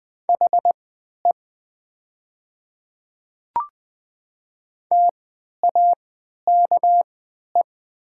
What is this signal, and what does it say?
Morse code, heard as a keyed tone, about 20 wpm, spaced as for 11 wpm, 700 Hz. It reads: HE TAKE
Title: 3x Character Spacing - Top 200 Words - 20wpm